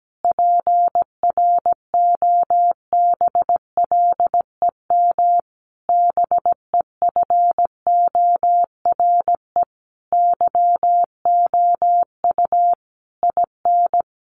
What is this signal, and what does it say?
Morse code, heard as a keyed tone, about 17 wpm, 700 Hz. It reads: PROBLEM BEFORE YOU IN
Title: Sentences from Top 1500 Words 17wpm